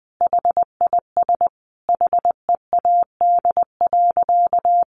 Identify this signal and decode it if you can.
Morse code, heard as a keyed tone, about 20 wpm, 700 Hz. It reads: HIS HEAD.
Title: Sentences from Top 1100 Words 20wpm